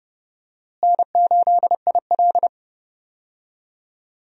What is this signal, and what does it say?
Morse code, heard as a keyed tone, about 32 wpm, 700 Hz. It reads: N8IL
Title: USA Call Signs - 4 Characters - Repeated 2x - 30wpm